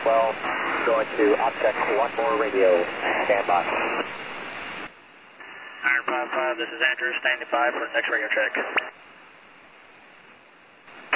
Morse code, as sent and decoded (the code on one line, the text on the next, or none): none